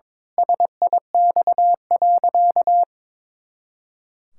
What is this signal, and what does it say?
Morse code, heard as a keyed tone, about 22 wpm, 700 Hz. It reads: SIX.
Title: Pangram Challenge v2 - 22wpm